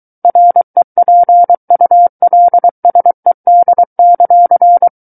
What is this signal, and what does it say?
Morse code, heard as a keyed tone, about 23 wpm, 700 Hz. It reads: REPULSED;
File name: Book One / 14-Chapter 14 - 23 WPM